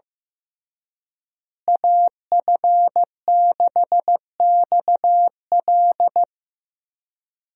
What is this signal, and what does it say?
Morse code, heard as a keyed tone, about 15 wpm, 700 Hz. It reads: AF6XL
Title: USA Call Signs - 5 Characters - Repeated 3x - 15wpm